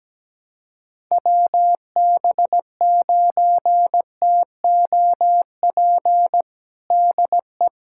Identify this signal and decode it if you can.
Morse code, heard as a keyed tone, about 17 wpm, 700 Hz. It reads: WB9TOP DE
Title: USA Callsign DE Callsign 17wpm